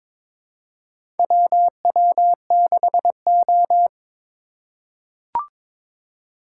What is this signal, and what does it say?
Morse code, heard as a keyed tone, about 22 wpm, 700 Hz. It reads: WW6O
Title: USA Call Signs - 4 Characters - 4x Character Spacing - 22wpm